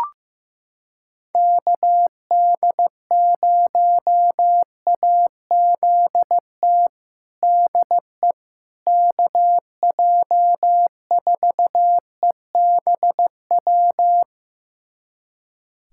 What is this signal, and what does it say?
Morse code, heard as a keyed tone, about 15 wpm, 700 Hz. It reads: KD0AZT DE KJ4EBW
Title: USA Callsign DE Callsign 15wpm